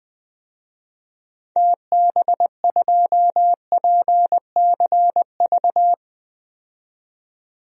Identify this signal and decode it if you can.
Morse code, heard as a keyed tone, about 20 wpm, 700 Hz. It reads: TB2PCV
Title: DXCC-100 Call Signs - 6 Characters - 20wpm